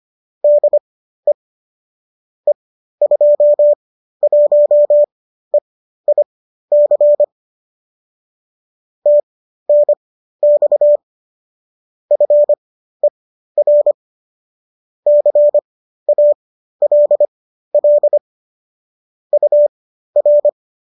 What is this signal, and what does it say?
Morse code, heard as a keyed tone, about 25 wpm, spaced as for 13 wpm, 590 Hz. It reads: DE E21EIC TNX FER CALL UR